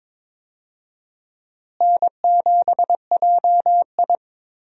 Morse code, heard as a keyed tone, about 22 wpm, 700 Hz. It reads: N7JI
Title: USA Call Signs - 4 Characters - Repeated 3x - 22wpm